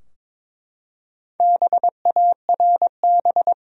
Morse code, heard as a keyed tone, about 22 wpm, 700 Hz. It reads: BARB
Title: USA Common Names - Max 5 Letters - 22wpm